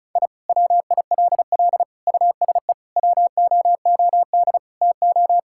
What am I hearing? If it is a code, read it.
Morse code, heard as a keyed tone, about 36 wpm, 700 Hz. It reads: I WILL USE WOOD TO
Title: Sentences from Top 1000 Words - Review Part 6 - 35wpm